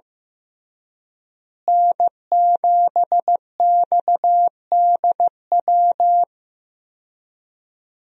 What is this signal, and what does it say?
Morse code, heard as a keyed tone, about 15 wpm, 700 Hz. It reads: N7XDW
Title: USA Call Signs - 5 Characters - Repeated 3x - 15wpm